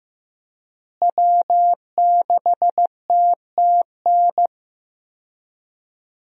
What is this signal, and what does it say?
Morse code, heard as a keyed tone, about 15 wpm, 700 Hz. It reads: W6TTN
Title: USA Call Signs - 5 Characters - Repeated 2x - 15wpm